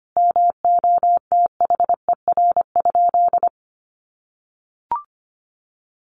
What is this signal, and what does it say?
Morse code, heard as a keyed tone, about 25 wpm, 700 Hz. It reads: MOTHER?
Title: Sentences from Top 1300 Words 25wpm